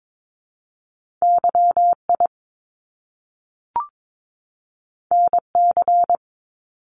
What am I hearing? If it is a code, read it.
Morse code, heard as a keyed tone, about 22 wpm, 700 Hz. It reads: YI NC
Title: Top Two Letter Combinations in Words 22wpm